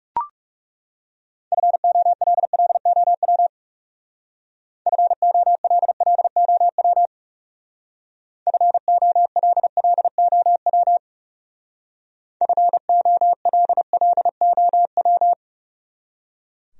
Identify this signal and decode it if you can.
Morse code, heard as a keyed tone, about 40 wpm, 700 Hz. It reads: FOLLOW FOLLOW FOLLOW FOLLOW